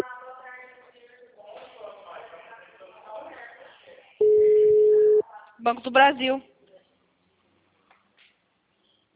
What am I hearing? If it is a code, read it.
Ringback tone (Europe, 425 Hz), with speech.
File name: out- -2029- - - .14109